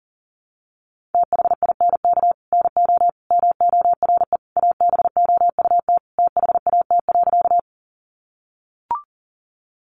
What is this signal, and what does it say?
Morse code, heard as a keyed tone, about 40 wpm, 700 Hz. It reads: THINK NO MORE ABOUT THAT.
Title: Sentences from Top 200 Words 40wpm